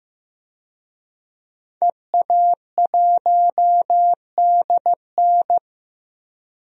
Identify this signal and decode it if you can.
Morse code, heard as a keyed tone, about 15 wpm, 700 Hz. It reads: EA1DN
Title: DXCC-100 Call Signs - 5 Characters - 15wpm